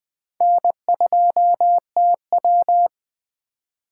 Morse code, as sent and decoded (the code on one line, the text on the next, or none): -. ..--- - .--
N2TW